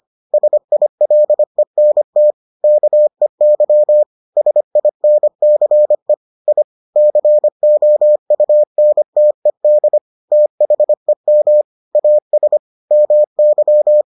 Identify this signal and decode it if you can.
Morse code, heard as a keyed tone, about 25 wpm, 600 Hz. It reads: SILENT KEY SINCE I COUNTED THEM AS MY